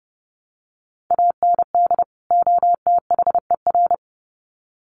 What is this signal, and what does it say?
Morse code, heard as a keyed tone, about 30 wpm, 700 Hz. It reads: AND OTHER
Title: Sets of 2 Words 30wpm